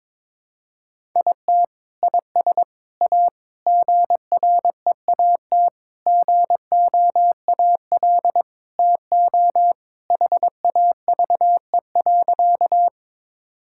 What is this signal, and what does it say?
Morse code, heard as a keyed tone, about 22 wpm, 700 Hz. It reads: IT IS A GREAT GOAL TO HAVE.
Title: Sentences from Top 1000 Words - Review Part 5 - 22wpm